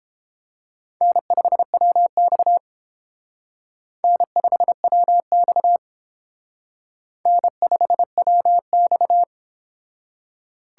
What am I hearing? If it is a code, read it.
Morse code, heard as a keyed tone, about 29 wpm, 700 Hz. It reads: N5WX N5WX N5WX